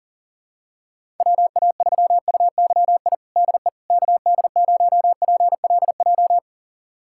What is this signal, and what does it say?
Morse code, heard as a keyed tone, about 37 wpm, 700 Hz. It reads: WA3UYI DE KD0PLJ
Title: USA Callsign DE Callsign - 6 Characters - 40wpm